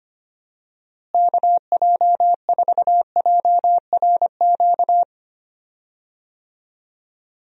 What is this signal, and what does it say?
Morse code, heard as a keyed tone, about 25 wpm, 700 Hz. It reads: KJ4JRQ
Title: USA Call Signs - 6 Characters - Repeated 2x - 25wpm